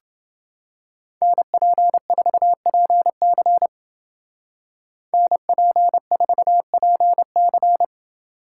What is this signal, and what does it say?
Morse code, heard as a keyed tone, about 28 wpm, 700 Hz. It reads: NP4PC NP4PC